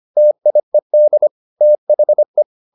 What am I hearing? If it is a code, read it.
Morse code, heard as a keyed tone, about 25 wpm, 600 Hz. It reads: TIED THE